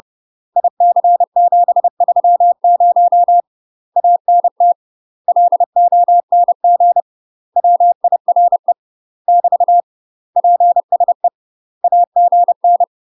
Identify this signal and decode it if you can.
Morse code, heard as a keyed tone, about 30 wpm, 700 Hz. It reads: IC730 ANT LONG WIRE <BT> PSE AGN
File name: UK MASTER QSOs TRAINING / 30WPM / UK MASTER QSOs 0009